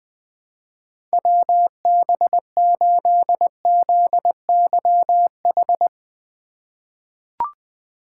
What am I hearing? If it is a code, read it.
Morse code, heard as a keyed tone, about 20 wpm, 700 Hz. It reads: WB8ZYH
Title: USA Call Signs - 6 Characters - 4x Character Spacing - 20wpm